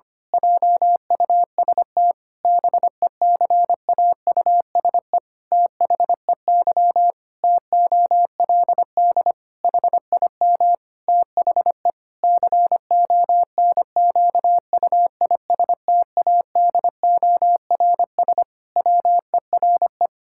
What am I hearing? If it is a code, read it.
Morse code, heard as a keyed tone, about 25 wpm, 700 Hz. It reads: JUST BECAUSE THEY TOLD HIM THE CONQUISTADORS WERE
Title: Pangram Challenge v2 - 25wpm